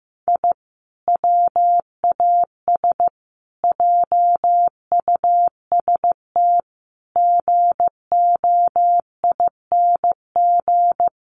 Morse code, heard as a keyed tone, about 15 wpm, 700 Hz. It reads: I WAS JUST GOING
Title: Sets of 5 Words Encore 15wpm